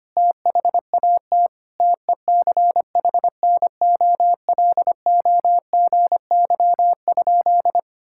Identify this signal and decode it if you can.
Morse code, heard as a keyed tone, about 25 wpm, 700 Hz. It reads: THAT TECHNOLOGY?
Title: Sentences from Top 1000 Words - Review Part 5 - 25wpm